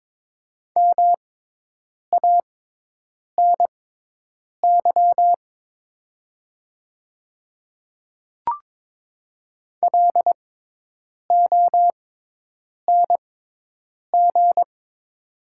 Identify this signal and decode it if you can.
Morse code, heard as a keyed tone, about 22 wpm, spaced as for 8 wpm, 700 Hz. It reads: MANY LONG
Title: 6x Character Spacing - Top 200 Words - 22wpm